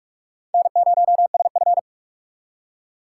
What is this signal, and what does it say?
Morse code, heard as a keyed tone, about 57 wpm, 700 Hz. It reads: N0SF